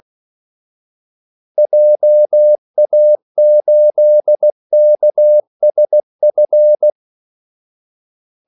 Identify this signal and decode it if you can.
Morse code, heard as a keyed tone, about 16 wpm, 600 Hz. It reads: JA8KSF
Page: Kaldesignaler 16 wpm